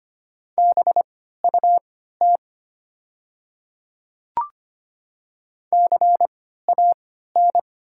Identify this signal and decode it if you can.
Morse code, heard as a keyed tone, about 25 wpm, spaced as for 14 wpm, 700 Hz. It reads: BUT CAN
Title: 3x Character Spacing - Top 100 Words - 25wpm